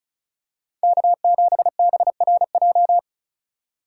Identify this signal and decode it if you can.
Morse code, heard as a keyed tone, about 36 wpm, 700 Hz. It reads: K7BRJ
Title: USA Call Signs - 5 Characters - 2x Character Spacing - 35wpm